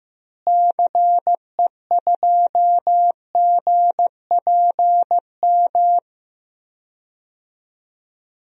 Morse code, heard as a keyed tone, about 15 wpm, 700 Hz. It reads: CE2GPM
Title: DXCC-100 Call Signs - 6 Characters - Repeated 3x - 15wpm